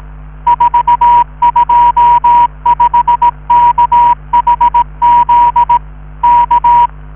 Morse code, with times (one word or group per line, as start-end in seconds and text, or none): 0.5-5.8 425KHZ
6.2-6.9 K